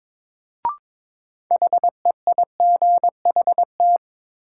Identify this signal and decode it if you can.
Morse code, heard as a keyed tone, about 22 wpm, 700 Hz. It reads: HEIGHT
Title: Sentences from Top 1600 Words - New Word Drill Alt - 22wpm